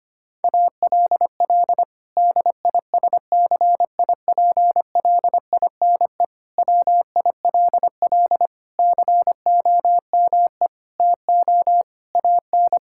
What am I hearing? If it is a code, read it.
Morse code, heard as a keyed tone, about 25 wpm, 700 Hz. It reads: ALL DISCIPLINE WILL COME TO AN
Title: Sentences from Top 1000 Words - Review Part 6 - 25wpm